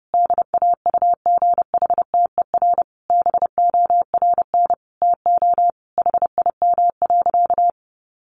Morse code, heard as a keyed tone, about 30 wpm, 700 Hz. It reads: DAUGHTER BORN TO HIM.
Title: Sentences from Top 1500 Words 30wpm